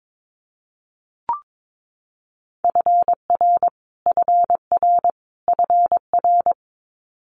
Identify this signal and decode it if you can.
Morse code, heard as a keyed tone, about 22 wpm, 700 Hz. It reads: FR FR FR
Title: Multi-letter Phonemes Repeated 3x 22wpm